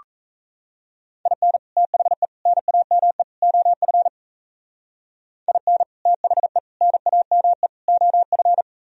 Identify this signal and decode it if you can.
Morse code, heard as a keyed tone, about 33 wpm, 700 Hz. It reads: IN THE NAME OF IN THE NAME OF
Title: Speed-Racing - Sets of 4 Words 28wpm